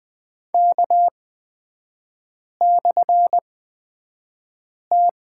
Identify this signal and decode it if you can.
Morse code, heard as a keyed tone, about 20 wpm, spaced as for 5 wpm, 690 Hz. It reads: K/T
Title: 229 Lesson 37e - Letter x - Three Characters 20wpm